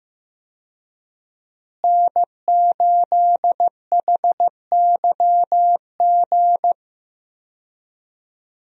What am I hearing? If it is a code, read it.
Morse code, heard as a keyed tone, about 14 wpm, 700 Hz. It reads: N8HYG
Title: USA Call Signs - 5 Characters - Repeated 2x - 15wpm